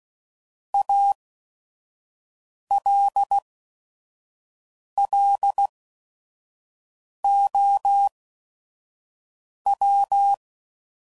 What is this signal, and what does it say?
Morse code, heard as a keyed tone, about 16 wpm, spaced as for 5 wpm, 800 Hz. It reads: ALLOW
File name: CW Code Groups @5WPM